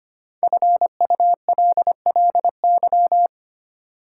Morse code, heard as a keyed tone, about 25 wpm, 700 Hz. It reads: FULLY
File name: 5 Letter Words - First 3 Letters Given - 25wpm